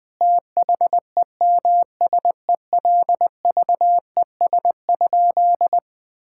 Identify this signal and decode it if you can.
Morse code, heard as a keyed tone, about 21 wpm, 700 Hz. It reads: THEMSELVES?
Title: Sentences from Top 1600 Words 20wpm